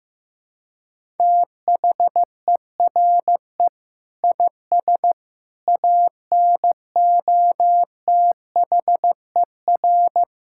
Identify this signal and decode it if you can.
Morse code, heard as a keyed tone, about 16 wpm, 700 Hz. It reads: THERE IS ANOTHER
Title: Sentences from Top 1000 Words - Review Part 5 - 15wpm